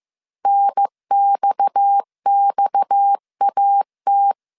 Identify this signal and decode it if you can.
Morse code, heard as a keyed tone, about 15 wpm, 800 Hz. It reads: NXXAT
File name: CW / DET / 1 6 1500C S C 14wpm 2951